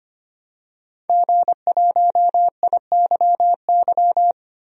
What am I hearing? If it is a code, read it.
Morse code, heard as a keyed tone, about 25 wpm, 700 Hz. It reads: G1IYY
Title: DXCC-100 Call Signs - 5 Characters - 25wpm